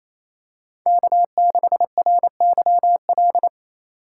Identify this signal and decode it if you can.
Morse code, heard as a keyed tone, about 28 wpm, 700 Hz. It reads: K6RYL
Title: USA Call Signs - 5 Characters - 2x Character Spacing - 28wpm